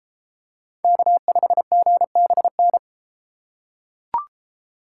Morse code, heard as a keyed tone, about 31 wpm, 700 Hz. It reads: K5GBN